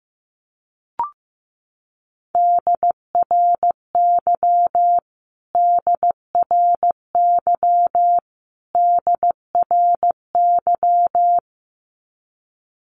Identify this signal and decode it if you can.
Morse code, heard as a keyed tone, about 15 wpm, 700 Hz. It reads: DRY DRY DRY
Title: Top 500 Words Repeated 15wpm